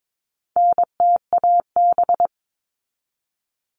Morse code, heard as a keyed tone, about 22 wpm, 700 Hz. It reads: NTAB
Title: All Cut Numbers - 4-Digit 22wpm